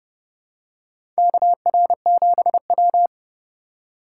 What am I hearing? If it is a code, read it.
Morse code, heard as a keyed tone, about 32 wpm, 700 Hz. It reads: KR7W
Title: K1USN SST QSO Practice - 30wpm